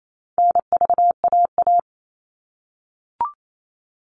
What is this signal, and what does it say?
Morse code, heard as a keyed tone, about 28 wpm, 700 Hz. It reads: NVAA
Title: All Cut Numbers - 4-Digit 28wpm